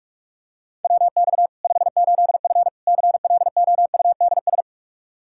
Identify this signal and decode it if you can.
Morse code, heard as a keyed tone, about 46 wpm, 700 Hz. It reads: WX 58F CLOUDS
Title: Practice QSO - Exchange 2 - Info - 45wpm